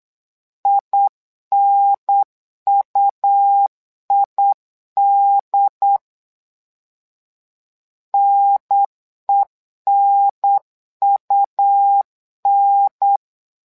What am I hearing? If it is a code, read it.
Morse code, heard as a keyed tone, about 8 wpm, 800 Hz. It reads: INUID NENUN
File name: LEZIONI / 2 UHDN / 2 UHDN 8wpm2232